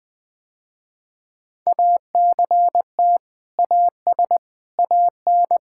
Morse code, heard as a keyed tone, about 20 wpm, 700 Hz. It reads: ACT AS AN
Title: Sentences with 2-Word Phrases - Spoken Beginning - 20wpm